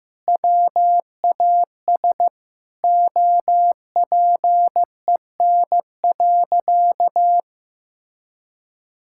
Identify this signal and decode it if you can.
Morse code, heard as a keyed tone, about 15 wpm, 700 Hz. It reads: WAS OPEN.
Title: Sentences from Top 1600 Words 15wpm